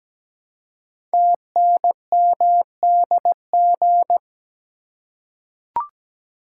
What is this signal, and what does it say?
Morse code, heard as a keyed tone, about 17 wpm, 700 Hz. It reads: TNMDG
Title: Code Groups - 5 Letters - 17wpm